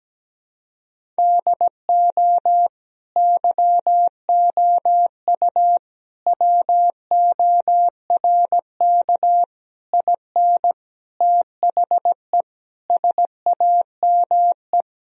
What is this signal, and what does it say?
Morse code, heard as a keyed tone, about 17 wpm, 700 Hz. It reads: DO YOU WORK IN THE SAME
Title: Sentences from Top 1000 Words - Review Part 4 - 17wpm